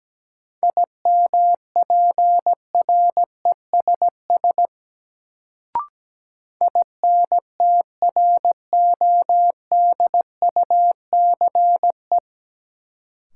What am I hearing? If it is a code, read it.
Morse code, heard as a keyed tone, about 17 wpm, 700 Hz. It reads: IMPRESS INTRODUCE
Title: Sentences from Top 1600 Words - New Word Drill - 17wpm